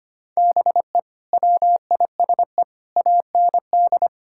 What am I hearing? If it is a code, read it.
Morse code, heard as a keyed tone, about 25 wpm, 700 Hz. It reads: BE WISE AND
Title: Sentences from Top 1600 Words 25wpm